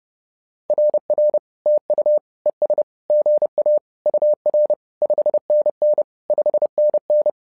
Morse code, heard as a keyed tone, about 30 wpm, 600 Hz. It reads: RR TU ES GA UR 5NN 5NN
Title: SOTA QSO Practice 30wpm